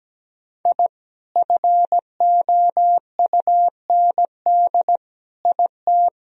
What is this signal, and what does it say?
Morse code, heard as a keyed tone, about 17 wpm, 700 Hz. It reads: I FOUND IT
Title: Sentences from Top 1000 Words - Review Part 2 - 17wpm